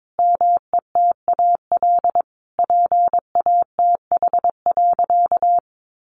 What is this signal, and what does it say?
Morse code, heard as a keyed tone, about 22 wpm, 700 Hz. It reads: METAL PATH.
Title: A Sound Of Thunder 22wpm